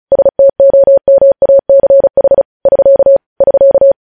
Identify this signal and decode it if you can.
Morse code, heard as a keyed tone, about 36 wpm, 550 Hz. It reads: STOMACH <SK> <SK>